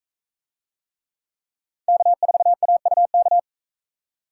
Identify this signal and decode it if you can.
Morse code, heard as a keyed tone, about 40 wpm, 700 Hz. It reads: K4AUK